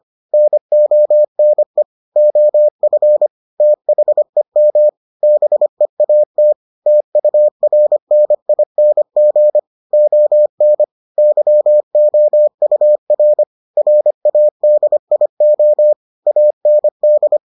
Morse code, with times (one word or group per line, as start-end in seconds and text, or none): 0.3-1.8 NONE
2.2-3.3 OF
3.6-4.9 THEM
5.2-6.5 BEAT
6.9-9.6 TURNING
9.9-10.8 ON
11.2-13.4 YOUR
13.8-15.9 RADIO
16.3-17.4 AND